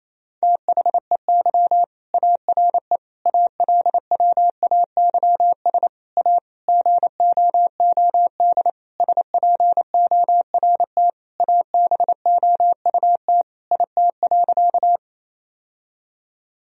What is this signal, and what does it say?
Morse code, heard as a keyed tone, about 28 wpm, 700 Hz. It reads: THEY ARE ALWAYS A GOOD SPORT ABOUT IT.
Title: Sentences from Top 1000 Words - Review Part 4 - 28wpm